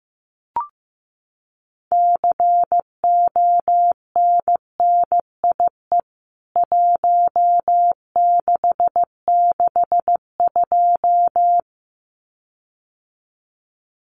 Morse code, heard as a keyed tone, about 15 wpm, 700 Hz. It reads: CONNIE 1662
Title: CWT Exchange 15wpm